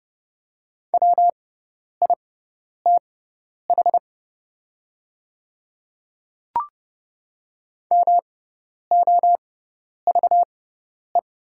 Morse code, heard as a keyed tone, about 30 wpm, spaced as for 10 wpm, 700 Hz. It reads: WITH MOVE
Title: 6x Character Spacing - Top 500 Words - 30wpm